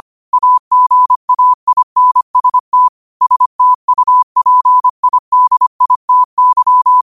Morse code, heard as a keyed tone, about 25 wpm, 1000 Hz. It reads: AGAINST STUPIDITY